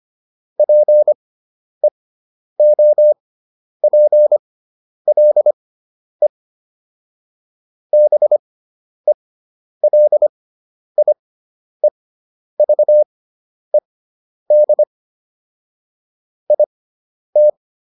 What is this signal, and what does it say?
Morse code, heard as a keyed tone, about 25 wpm, spaced as for 10 wpm, 600 Hz. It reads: PEOPLE BELIEVED IT